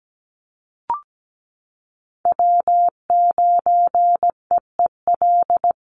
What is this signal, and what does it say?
Morse code, heard as a keyed tone, about 17 wpm, 700 Hz. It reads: W9EEL
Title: USA Call Signs Repeated 2x 17wpm